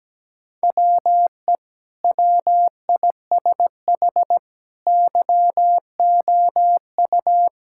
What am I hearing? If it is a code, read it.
Morse code, heard as a keyed tone, about 17 wpm, 700 Hz. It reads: WE WISH YOU
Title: Sentences from Top 1600 Words 17wpm